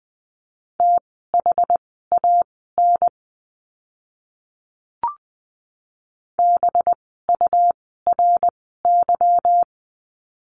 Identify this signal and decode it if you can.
Morse code, heard as a keyed tone, about 20 wpm, spaced as for 14 wpm, 700 Hz. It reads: THAN BURY
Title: 185 Lesson 30f - Letter b - Words 20wpm